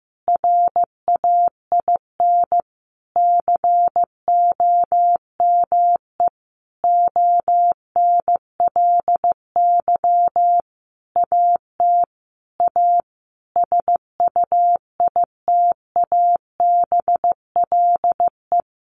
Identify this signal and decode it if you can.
Morse code, heard as a keyed tone, about 15 wpm, 700 Hz. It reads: RAIN COME ONLY AT A SUITABLE
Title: Sentences from Top 1400 Words 15wpm